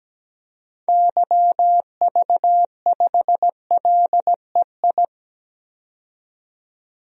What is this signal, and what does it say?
Morse code, heard as a keyed tone, about 17 wpm, 700 Hz. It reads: YV5LEI
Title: DXCC-100 Call Signs - 6 Characters - 17wpm